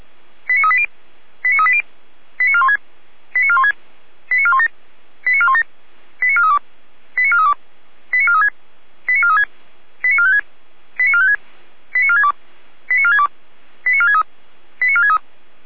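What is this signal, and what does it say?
Five-tone selective call (ZVEI), heard as quick runs of five tones, tones of about 70 ms. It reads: 89380 89380 89416 89416 89417 89417 89432 89432 89436 89436 89456 89456 89462 89462 89463 89463
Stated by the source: Five-tone ZVEI selective call received on SDR.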